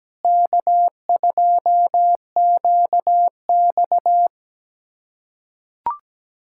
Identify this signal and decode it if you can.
Morse code, heard as a keyed tone, about 17 wpm, 700 Hz. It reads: K2QX
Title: USA Callsign DE Callsign - 4 Characters - 17wpm